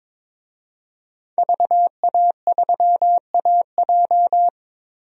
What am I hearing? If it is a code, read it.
Morse code, heard as a keyed tone, about 22 wpm, 700 Hz. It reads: VA3AJ
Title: DXCC-100 Call Signs - 5 Characters - Repeated 3x - 22wpm